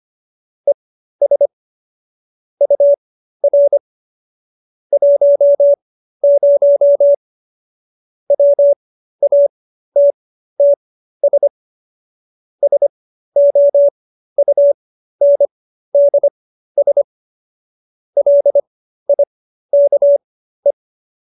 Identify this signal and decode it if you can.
Morse code, heard as a keyed tone, about 25 wpm, spaced as for 13 wpm, 570 Hz. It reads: ES UR 10 WATTS SOUNDS LIKE